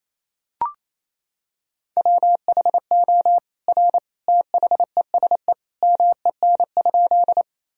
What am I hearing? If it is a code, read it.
Morse code, heard as a keyed tone, about 28 wpm, 700 Hz. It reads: WHO R THESE MEN?
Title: Sentences from Top 200 Words and Abbreviations 28wpm